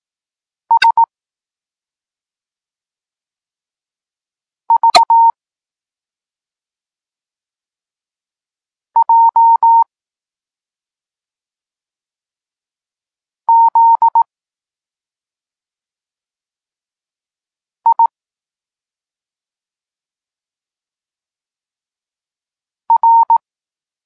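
Morse code, heard as a keyed tone, about 18 wpm, 920 Hz. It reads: S V J Z I R
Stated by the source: Morse code lesson for learning the letters K and X, from the SP9MOA Niepołomicki Klub Krótkofalowców course.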